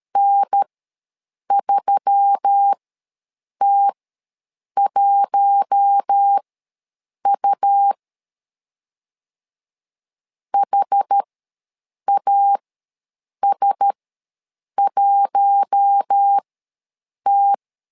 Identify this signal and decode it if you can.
Morse code, heard as a keyed tone, about 13 wpm, spaced as for 7 wpm, 800 Hz. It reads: N3T1U HAS1T
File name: LEZIONI / 4 Q1S3 ws 12wpm 2038